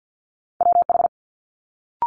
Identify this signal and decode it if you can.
Morse code, heard as a keyed tone, about 50 wpm, 700 Hz. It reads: WH
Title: Multi-letter Phonemes Quick Fire 50wpm